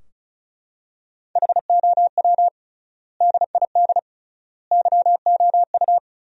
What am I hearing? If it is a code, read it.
Morse code, heard as a keyed tone, about 36 wpm, 700 Hz. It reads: HOW DID YOU
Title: 3x Word-Spacing - Sentences from Top 100 Words - 35wpm